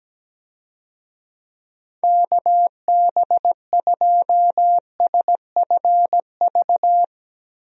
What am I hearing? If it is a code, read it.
Morse code, heard as a keyed tone, about 17 wpm, 700 Hz. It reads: KB2SFV
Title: USA Call Signs - 6 Characters - Repeated 2x - 17wpm